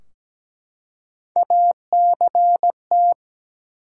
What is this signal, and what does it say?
Morse code, heard as a keyed tone, about 17 wpm, 700 Hz. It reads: ACT
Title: Three Letter Words - 17wpm